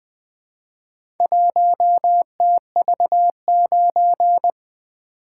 Morse code, heard as a keyed tone, about 20 wpm, 700 Hz. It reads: 1TV9
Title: Four Letter-Number Combinations - 20wpm